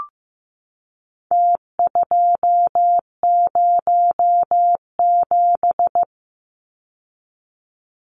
Morse code, heard as a keyed tone, about 15 wpm, 700 Hz. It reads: T207
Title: Numbers - Contest Style - 4-Digit 15wpm